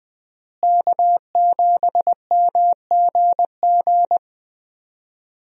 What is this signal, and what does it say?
Morse code, heard as a keyed tone, about 20 wpm, 700 Hz. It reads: K7MGG
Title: USA Call Signs - 5 Characters - Repeated 3x - 20wpm